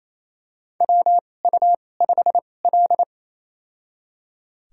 Morse code, heard as a keyed tone, about 28 wpm, spaced as for 20 wpm, 700 Hz. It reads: WU5L
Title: USA Call Signs - 4 Characters - 2x Character Spacing - 28wpm